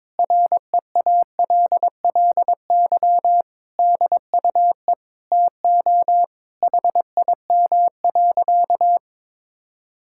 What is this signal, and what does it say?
Morse code, heard as a keyed tone, about 22 wpm, 700 Hz. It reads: REALLY DUE TO HIM.
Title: Sentences from Top 1000 Words - Review Part 6 - 22wpm